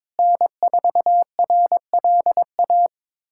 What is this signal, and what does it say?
Morse code, heard as a keyed tone, about 22 wpm, 700 Hz. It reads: N4RLA
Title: USA Callsign DE Callsign - 5 Characters - 22wpm